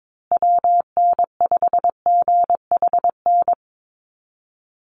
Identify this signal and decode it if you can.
Morse code, heard as a keyed tone, about 22 wpm, 700 Hz. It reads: WN5GHN
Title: USA Call Signs Repeated 2x 22wpm